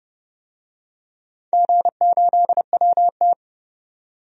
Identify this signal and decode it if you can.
Morse code, heard as a keyed tone, about 32 wpm, 700 Hz. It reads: G8WT
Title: DXCC-100 Call Signs - 4 Characters - 30wpm